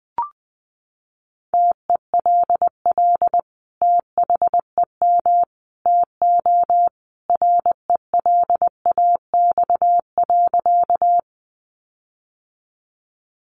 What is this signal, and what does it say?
Morse code, heard as a keyed tone, about 20 wpm, 700 Hz. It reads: TELL THEM TO RELAX.
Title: Sentences from Top 1400 Words 20wpm